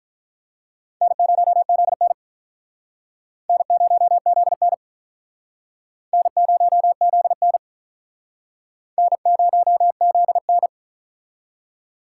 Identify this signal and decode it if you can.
Morse code, heard as a keyed tone, about 45 wpm, 700 Hz. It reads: N0ZN N0ZN N0ZN N0ZN